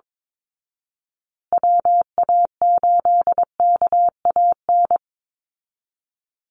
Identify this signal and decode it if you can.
Morse code, heard as a keyed tone, about 22 wpm, 700 Hz. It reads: WA8KAN
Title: Call Signs Repeated 3x - 22wpm